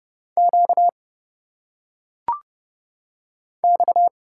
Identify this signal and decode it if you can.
Morse code, heard as a keyed tone, about 30 wpm, 700 Hz. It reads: Q X
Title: Single Letters - Phonetic Pronunciation 30wpm